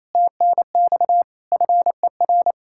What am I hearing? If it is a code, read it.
Morse code, heard as a keyed tone, about 34 wpm, 700 Hz. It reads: TNX FER
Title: Speed-Racing Common QSO Phrases Encore - 28wpm